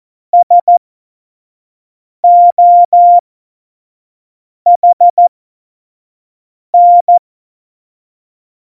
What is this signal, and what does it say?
Morse code, heard as a keyed tone, about 14 wpm, spaced as for 5 wpm, 700 Hz. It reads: SOHN
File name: Morsekurs / Witze / Witz 17 April 2018, 25 F70